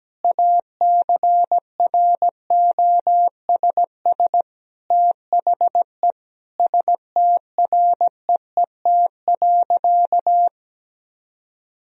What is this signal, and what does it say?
Morse code, heard as a keyed tone, about 17 wpm, 700 Hz. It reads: ACROSS THE STREET.
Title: Sentences from Top 1000 Words - Review Part 4 - 17wpm